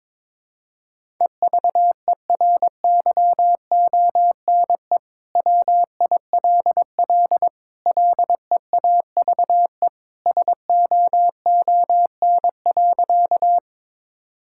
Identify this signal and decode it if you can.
Morse code, heard as a keyed tone, about 22 wpm, 700 Hz. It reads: EVERYONE WILL LEAVE SOON.
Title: Sentences from Top 1000 Words - Review Part 5 - 22wpm